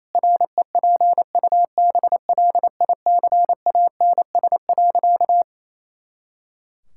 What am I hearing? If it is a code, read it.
Morse code, heard as a keyed tone, about 28 wpm, 700 Hz. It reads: REPUBLICANS.
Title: Pangram Challenge v2 - 28wpm